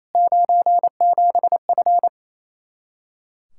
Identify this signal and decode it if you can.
Morse code, heard as a keyed tone, about 27 wpm, 700 Hz. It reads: 97F
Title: Practice QSO - Exchange 2 - Info - 28wpm